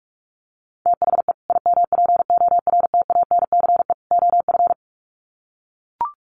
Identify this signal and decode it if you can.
Morse code, heard as a keyed tone, about 44 wpm, 700 Hz. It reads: THE IMPORTANCE OF
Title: Sets of 3 Words 45wpm